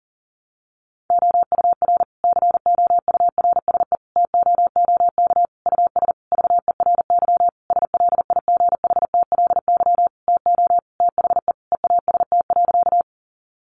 Morse code, heard as a keyed tone, about 40 wpm, 700 Hz. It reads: OUR COURSE TOOK US VERY SLIGHTLY TO THE EAST.
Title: Sentences from Top 1300 Words 40wpm